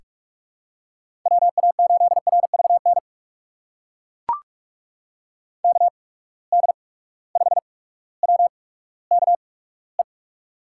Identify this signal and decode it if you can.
Morse code, heard as a keyed tone, about 42 wpm, spaced as for 19 wpm, 700 Hz. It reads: WA8RVN KD5WKE
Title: USA Call Signs - 6 Characters - 8x Character Spacing - 45wpm